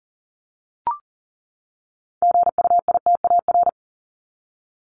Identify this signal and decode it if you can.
Morse code, heard as a keyed tone, about 40 wpm, 700 Hz. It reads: GUITAR
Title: Top 1501-1750 Words 40wpm